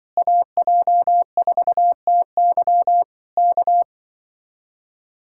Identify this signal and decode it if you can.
Morse code, heard as a keyed tone, about 25 wpm, 700 Hz. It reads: AJ4TY K
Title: Speed-Racing Common QSO Phrases - 20wpm